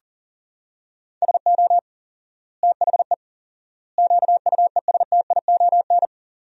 Morse code, heard as a keyed tone, about 37 wpm, 700 Hz. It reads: SO THE QUESTION
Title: 4x Word-Spacing - Sentences from Top 400 Words - 40wpm